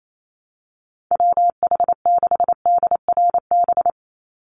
Morse code, heard as a keyed tone, about 28 wpm, 700 Hz. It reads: WH6DRB
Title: USA Call Signs 28wpm